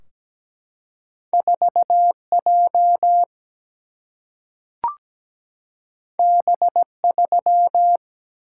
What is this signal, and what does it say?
Morse code, heard as a keyed tone, about 17 wpm, 700 Hz. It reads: 4 J B 3
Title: Single-Letter Single-Number 17wpm